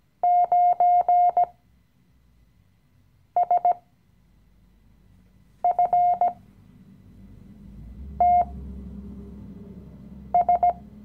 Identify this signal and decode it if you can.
Morse code, heard as a keyed tone, about 17 wpm, spaced as for 4 wpm, 710 Hz. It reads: 9SFTS
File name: CW-Online-Kurs / Anfänger / Lektion 015